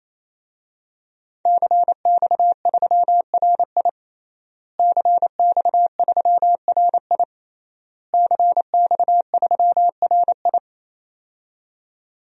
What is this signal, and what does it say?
Morse code, heard as a keyed tone, about 28 wpm, 700 Hz. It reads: CX3RI CX3RI CX3RI
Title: DXCC-100 Call Signs - 5 Characters - Repeated 3x - 28wpm